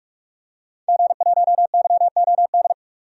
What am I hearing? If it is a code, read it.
Morse code, heard as a keyed tone, about 41 wpm, 700 Hz. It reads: G1YOD
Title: DXCC-100 Call Signs - 5 Characters - 45wpm